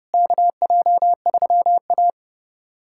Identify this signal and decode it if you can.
Morse code, heard as a keyed tone, about 32 wpm, 700 Hz. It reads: KJ3A
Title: USA Callsign DE Callsign - 4 Characters - 30wpm